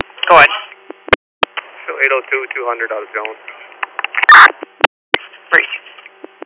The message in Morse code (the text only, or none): none